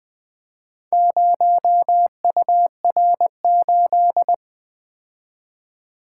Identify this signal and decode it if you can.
Morse code, heard as a keyed tone, about 20 wpm, 700 Hz. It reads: 0UR8
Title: Four Letter-Number Combinations - 20wpm